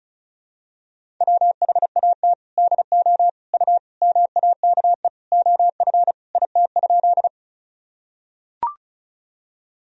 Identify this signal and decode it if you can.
Morse code, heard as a keyed tone, about 36 wpm, 700 Hz. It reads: WHAT DO U MAKE OF IT?
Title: Sentences from Top 100 Words and Abbreviations 35wpm